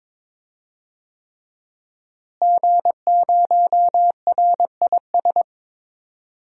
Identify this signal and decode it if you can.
Morse code, heard as a keyed tone, about 22 wpm, 700 Hz. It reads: G0RIS
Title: DXCC-100 Call Signs - 5 Characters - Repeated 3x - 22wpm